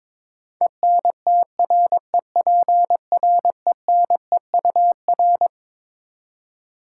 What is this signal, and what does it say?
Morse code, heard as a keyed tone, about 22 wpm, 700 Hz. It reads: ENTREPRENEUR
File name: Twelve Letter Words - 22wpm